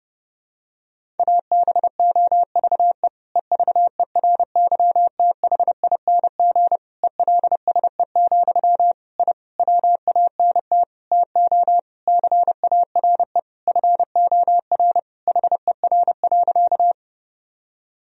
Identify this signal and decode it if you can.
Morse code, heard as a keyed tone, about 30 wpm, 700 Hz. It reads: ABOVE EVERYTHING ELSE, I WANT TO CARE FOR HER.
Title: Sentences from Top 1000 Words - Review Part 2 - 30wpm